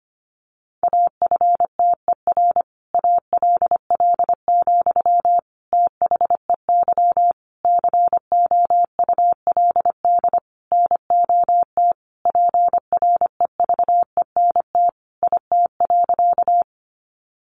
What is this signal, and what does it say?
Morse code, heard as a keyed tone, about 25 wpm, 700 Hz. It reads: AFTER ALL, THEY COULD NOT PREVENT IT.
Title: Sentences from Top 1100 Words 25wpm